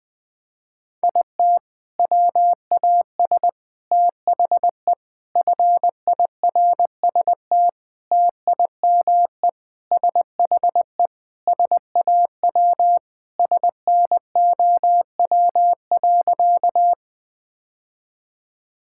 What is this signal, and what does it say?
Morse code, heard as a keyed tone, about 20 wpm, 700 Hz. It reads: IT WAS THE FIRST TIME SHE SAW SNOW.
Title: Sentences with 5-Word Phrases - 20wpm